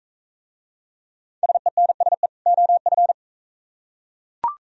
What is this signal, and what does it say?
Morse code, heard as a keyed tone, about 42 wpm, 700 Hz. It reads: SENSE OF